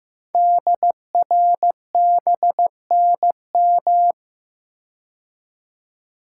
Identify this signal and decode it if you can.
Morse code, heard as a keyed tone, about 15 wpm, 700 Hz. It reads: DRBNM
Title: Code Groups - 5 Letters - 15wpm